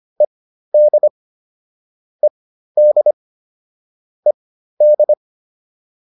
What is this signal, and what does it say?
Morse code, heard as a keyed tone, about 25 wpm, spaced as for 13 wpm, 600 Hz. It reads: ED ED ED